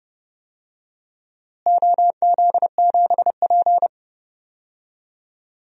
Morse code, heard as a keyed tone, about 30 wpm, 700 Hz. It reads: OZ7P
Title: DXCC-100 Call Signs - 4 Characters - Repeated 2x - 30wpm